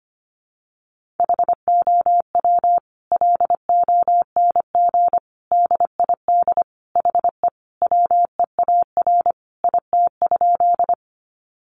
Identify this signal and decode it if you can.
Morse code, heard as a keyed tone, about 25 wpm, 700 Hz. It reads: HOW LONG DID HE WEAR IT?
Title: Sentences from Top 1500 Words 25wpm